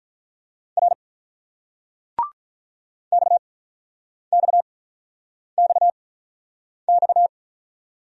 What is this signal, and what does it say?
Morse code, heard as a keyed tone, about 48 wpm, spaced as for 9 wpm, 690 Hz. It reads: R XXXX